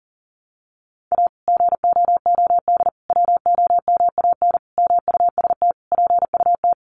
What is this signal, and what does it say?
Morse code, heard as a keyed tone, about 40 wpm, 700 Hz. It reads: A GOOD WOMAN MUST PUT
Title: Sentences from Top 1100 Words 40wpm